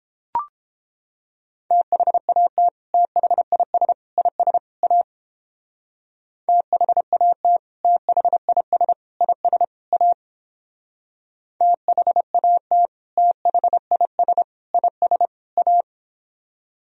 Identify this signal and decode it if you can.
Morse code, heard as a keyed tone, about 29 wpm, 700 Hz. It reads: THAT THIS IS A THAT THIS IS A THAT THIS IS A